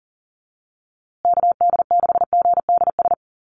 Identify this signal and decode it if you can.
Morse code, heard as a keyed tone, about 40 wpm, 700 Hz. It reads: KD6GDS
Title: USA Call Signs Repeated 3x 40wpm